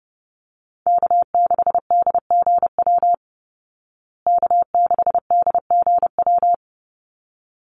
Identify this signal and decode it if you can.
Morse code, heard as a keyed tone, about 28 wpm, 700 Hz. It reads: K6DGW K6DGW
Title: Call Signs Repeated 3x - 30wpm